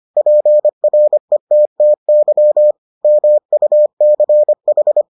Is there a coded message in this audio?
Yes: Morse code from 0.2 to 5.0 s, heard as a keyed tone, about 25 wpm, 600 Hz. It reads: PRETTY MUCH